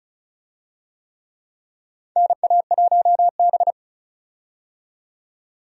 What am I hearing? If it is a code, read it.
Morse code, heard as a keyed tone, about 34 wpm, 700 Hz. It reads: NA1B
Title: USA Call Signs - 4 Characters - 35wpm